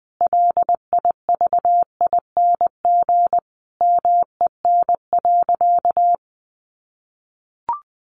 Morse code, heard as a keyed tone, about 20 wpm, 700 Hz. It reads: LIVING MEN.
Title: Sentences from Top 1300 Words 20wpm